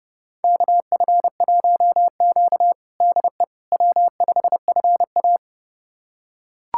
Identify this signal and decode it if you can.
Morse code, heard as a keyed tone, about 30 wpm, 700 Hz. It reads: KF1Q DE W5FA
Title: USA Callsign DE Callsign - 4 Characters - 30wpm